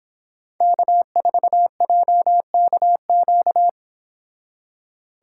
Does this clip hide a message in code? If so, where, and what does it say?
Yes: Morse code from 0.6 to 3.7 s, heard as a keyed tone, about 26 wpm, 700 Hz. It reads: K4JKQ